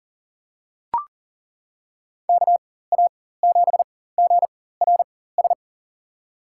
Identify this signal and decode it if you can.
Morse code, heard as a keyed tone, about 40 wpm, spaced as for 19 wpm, 700 Hz. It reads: KA7GRS
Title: USA Call Signs - 6 Characters - 4x Character Spacing - 40wpm